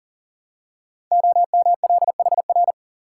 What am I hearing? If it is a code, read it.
Morse code, heard as a keyed tone, about 42 wpm, 690 Hz. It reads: OMLHR
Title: Code Groups - 5 Letters - 40wpm